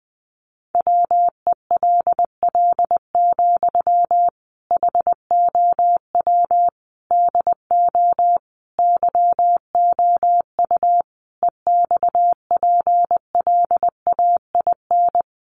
Morse code, heard as a keyed tone, about 20 wpm, 700 Hz. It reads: WELL, HOW DO YOU EXPLAIN
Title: Sentences from Top 1100 Words 20wpm